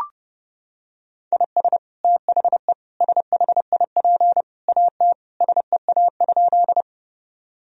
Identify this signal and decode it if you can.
Morse code, heard as a keyed tone, about 32 wpm, 700 Hz. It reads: IS THE SHIP AT SEA?
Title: Sentences from Top 1000 Words - Review Part 2 - 30wpm